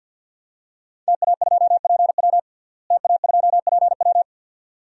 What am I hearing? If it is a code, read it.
Morse code, heard as a keyed tone, about 51 wpm, 700 Hz. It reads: TA2PW TA2PW
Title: DXCC-100 Call Signs - 5 Characters - Repeated 2x - 50wpm